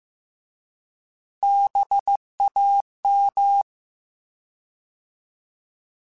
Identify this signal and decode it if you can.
Morse code, heard as a keyed tone, about 15 wpm, 800 Hz. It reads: BAM